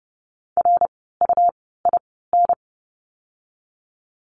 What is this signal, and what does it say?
Morse code, heard as a keyed tone, about 30 wpm, spaced as for 17 wpm, 700 Hz. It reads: RUIN
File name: Lesson 17f - Letter u - Words 30wpm